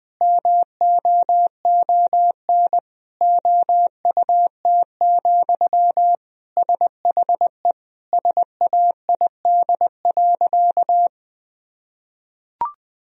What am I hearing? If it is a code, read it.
Morse code, heard as a keyed tone, about 20 wpm, 700 Hz. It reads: MOON OUT, SHE SAID.
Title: Sentences from Top 1000 Words - Review Part 3 - 20wpm